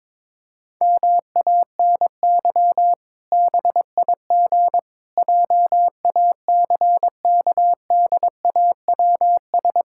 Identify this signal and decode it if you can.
Morse code, heard as a keyed tone, about 22 wpm, 700 Hz. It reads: MANY BIG JACKDAWS
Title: Pangram Challenge v2 - 22wpm